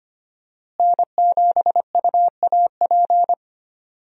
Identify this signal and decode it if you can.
Morse code, heard as a keyed tone, about 25 wpm, 700 Hz. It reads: N7UAP
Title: USA Call Signs - 5 Characters - Repeated 3x - 25wpm